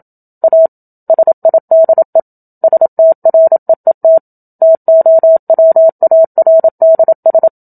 Morse code, heard as a keyed tone, about 27 wpm, 650 Hz. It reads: A SIDE STREET TOWARDS
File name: Book Two / 01-Chapter 1 - 27 WPM